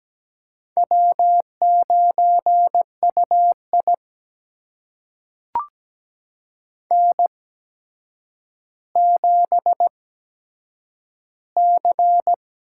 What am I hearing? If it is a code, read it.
Morse code, heard as a keyed tone, about 17 wpm, spaced as for 8 wpm, 700 Hz. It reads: W9UI N7C
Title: USA Call Signs - 4 Characters - 8x Character Spacing - 17wpm